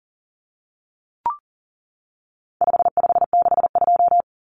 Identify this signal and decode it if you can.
Morse code, heard as a keyed tone, about 40 wpm, 700 Hz. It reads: 5562
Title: Common Cut Numbers - 4-Digit 40wpm